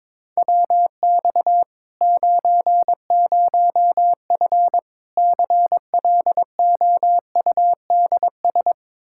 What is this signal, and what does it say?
Morse code, heard as a keyed tone, about 22 wpm, 700 Hz. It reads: WX 90F CLOUDS
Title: Practice QSO - Exchange 2 - Info - 22wpm